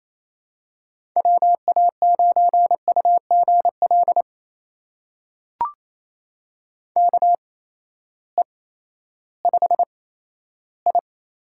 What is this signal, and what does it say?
Morse code, heard as a keyed tone, about 28 wpm, spaced as for 14 wpm, 700 Hz. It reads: WA9UGL KE5I
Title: USA Call Signs - 6 Characters - 8x Character Spacing - 28wpm